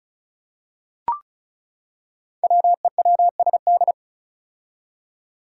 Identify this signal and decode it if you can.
Morse code, heard as a keyed tone, about 34 wpm, 700 Hz. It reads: WEWSD
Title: Five Challenging Letter-Number Combinations 35wpm